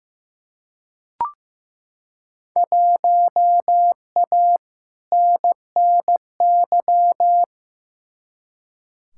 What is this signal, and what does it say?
Morse code, heard as a keyed tone, about 15 wpm, 700 Hz. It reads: 1A NNY
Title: Field Day Exchanges - 15wpm